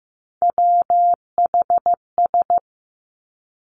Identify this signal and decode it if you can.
Morse code, heard as a keyed tone, about 15 wpm, 700 Hz. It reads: WHS
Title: Three Challenging Letter-Number Combinations 15wpm